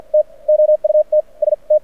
Morse code, heard as a keyed tone, about 48 wpm, spaced as for 34 wpm, 610 Hz. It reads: TOUT ST
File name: archived / CRSnet